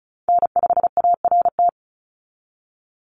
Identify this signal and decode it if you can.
Morse code, heard as a keyed tone, about 36 wpm, 700 Hz. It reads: N5ART
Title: USA Call Signs 35wpm